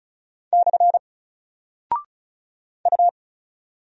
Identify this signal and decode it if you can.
Morse code, heard as a keyed tone, about 36 wpm, 700 Hz. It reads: / U
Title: Single Character (Letter:Number:Punctuation:Prosign) 35wpm